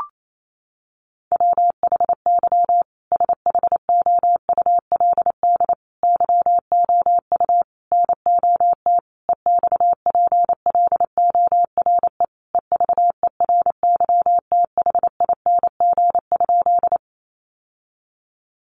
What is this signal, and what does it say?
Morse code, heard as a keyed tone, about 28 wpm, 700 Hz. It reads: WHY SHOULD YOU NOT EXPLORE EVERYTHING?
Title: Sentences from Top 1100 Words 28wpm